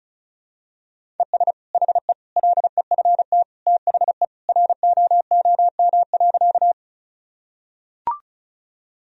Morse code, heard as a keyed tone, about 36 wpm, 700 Hz. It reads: ES HE LEFT THE ROOM.
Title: Sentences from Top 300 Words and Abbreviations 35wpm